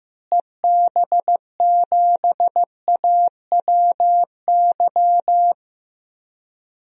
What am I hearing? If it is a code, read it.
Morse code, heard as a keyed tone, about 15 wpm, 700 Hz. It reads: EB7AWY
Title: DXCC-100 Call Signs - 6 Characters - Repeated 2x - 15wpm